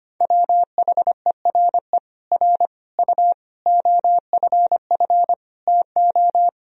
Morse code, heard as a keyed tone, about 25 wpm, 700 Hz. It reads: WHERE R U OFF TO
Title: Sentences from Top 400 Words and Abbreviations 25wpm